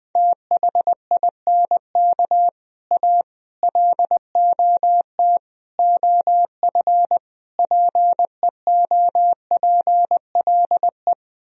Morse code, heard as a keyed tone, about 20 wpm, 700 Hz. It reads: THINK A LOT OF PEOPLE
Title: Speed-Racing - Sets of 5 Words 17wpm